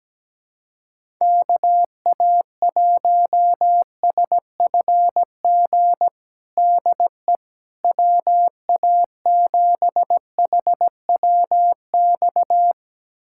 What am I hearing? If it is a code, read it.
Morse code, heard as a keyed tone, about 17 wpm, 700 Hz. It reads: KA1SFG DE WA7HWX
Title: USA Callsign DE Callsign 17wpm